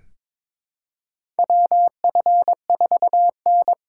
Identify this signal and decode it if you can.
Morse code, heard as a keyed tone, about 22 wpm, 700 Hz. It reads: WF4N
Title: USA Call Signs - 4 Characters - 2x Character Spacing - 22wpm